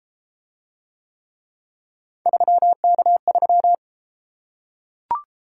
Morse code, heard as a keyed tone, about 33 wpm, 700 Hz. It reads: 3K3